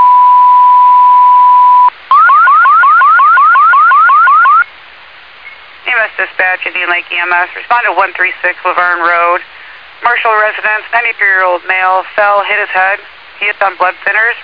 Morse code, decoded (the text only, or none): none